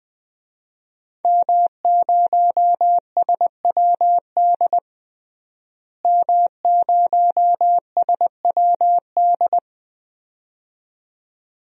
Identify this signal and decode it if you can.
Morse code, heard as a keyed tone, about 21 wpm, 700 Hz. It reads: M0SWD M0SWD
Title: DXCC-100 Call Signs - 5 Characters - Repeated 2x - 20wpm